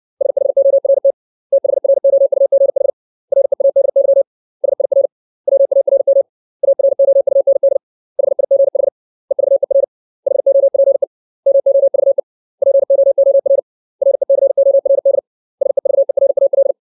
Morse code, heard as a keyed tone, about 60 wpm, 550 Hz. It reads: SHORT THROUGH READY SIR WARM AROUND HIGH EVER HOPE MOVE POOR RECORD SEVERAL
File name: 632 words / 632 Words 60WPM